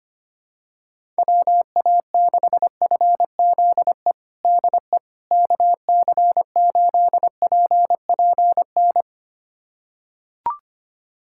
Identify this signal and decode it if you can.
Morse code, heard as a keyed tone, about 25 wpm, 700 Hz. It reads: WA6FZE DE KC8PPN
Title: USA Callsign DE Callsign - 6 Characters - 25wpm